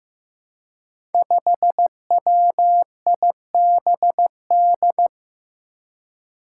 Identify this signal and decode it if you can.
Morse code, heard as a keyed tone, about 15 wpm, 700 Hz. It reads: 5WIBD
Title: Five Challenging Letter-Number Combinations 15wpm